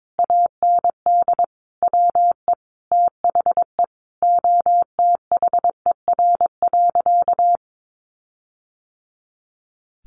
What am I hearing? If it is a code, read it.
Morse code, heard as a keyed tone, about 22 wpm, 700 Hz. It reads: AND WE THE OTHER.
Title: Sentences from Top 1300 Words 22wpm